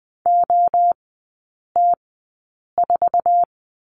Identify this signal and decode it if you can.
Morse code, heard as a keyed tone, about 20 wpm, spaced as for 8 wpm, 700 Hz. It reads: OT4
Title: 053 Lesson 10e - Letter r - Three Characters 20wpm